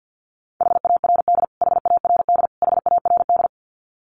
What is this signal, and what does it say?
Morse code, heard as a keyed tone, about 48 wpm, 700 Hz. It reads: HARD HARD HARD
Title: Top 500 Words Repeated 50wpm